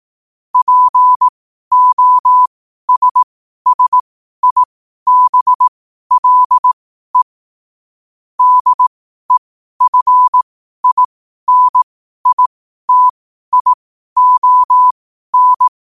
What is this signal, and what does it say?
Morse code, heard as a keyed tone, about 18 wpm, spaced as for 12 wpm, 1000 Hz. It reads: POSSIBLE DEFINITION
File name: qotdfiles / 12wpm